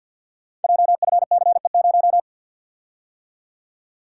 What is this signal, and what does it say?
Morse code, heard as a keyed tone, about 48 wpm, 700 Hz. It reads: JFCE0
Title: Code Groups - 5 Letter:Numbers - 50wpm